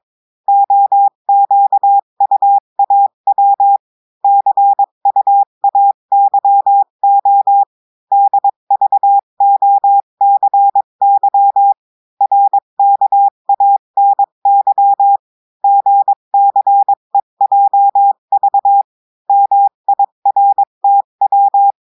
Morse code, heard as a keyed tone, about 22 wpm, 800 Hz. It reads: OQUAW CUAYO DVOCY RKANY GCEJV MIRTW
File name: cours-du-mercredi / audios / cours-853-du-2-mai-2024-etxte chiffre et texte clair-a-20-wpm